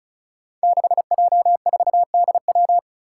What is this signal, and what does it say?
Morse code, heard as a keyed tone, about 36 wpm, 700 Hz. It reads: 6J4DW
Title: Five Challenging Letter-Number Combinations 35wpm